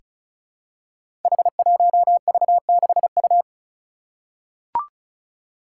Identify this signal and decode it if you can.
Morse code, heard as a keyed tone, about 36 wpm, 700 Hz. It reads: H1V6U
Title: Five Challenging Letter-Number Combinations 35wpm